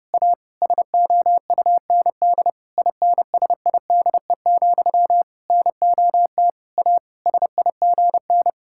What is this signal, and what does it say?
Morse code, heard as a keyed tone, about 32 wpm, 700 Hz. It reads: A SOUND INSIDE, NOT A SIGN
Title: Sentences from Top 1000 Words - Review Part 6 - 30wpm